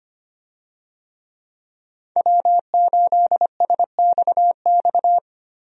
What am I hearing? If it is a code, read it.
Morse code, heard as a keyed tone, about 25 wpm, 700 Hz. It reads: W8SXX
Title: USA Call Signs - 5 Characters - Repeated 3x - 25wpm